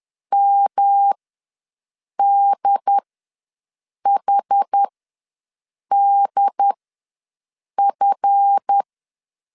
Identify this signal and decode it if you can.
Morse code, heard as a keyed tone, about 10 wpm, spaced as for 6 wpm, 800 Hz. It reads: MDHDF
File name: LEZIONI / 7 GF9Z / 7 GF ws 10wpm 3918